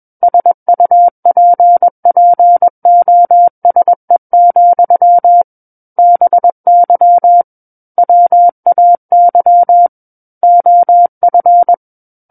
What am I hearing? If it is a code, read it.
Morse code, heard as a keyed tone, about 21 wpm, 700 Hz. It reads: SUPPOSE, BY WAY OF
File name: Book One / 12-Chapter 12 - 21 WPM